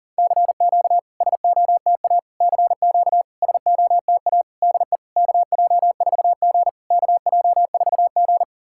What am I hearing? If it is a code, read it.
Morse code, heard as a keyed tone, about 40 wpm, 700 Hz. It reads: CQ SOTA CQ SOTA DE KJ4G KJ4G
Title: SOTA QSO Practice 40wpm